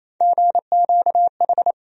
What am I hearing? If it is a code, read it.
Morse code, heard as a keyed tone, about 28 wpm, 700 Hz. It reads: GQH
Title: USA Call Sign Postfixes - 3 Letters 28wpm